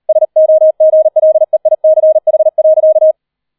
Morse code, heard as a keyed tone, about 38 wpm, 630 Hz. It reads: SOGLEICH.